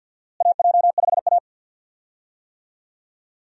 Morse code, heard as a keyed tone, about 50 wpm, 700 Hz. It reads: AJ5A
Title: USA Call Signs - 4 Characters - Repeated 2x - 50wpm